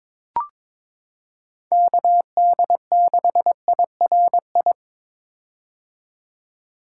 Morse code, heard as a keyed tone, about 22 wpm, 700 Hz. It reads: KD6IRI
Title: K1USN SST QSO Practice - 22wpm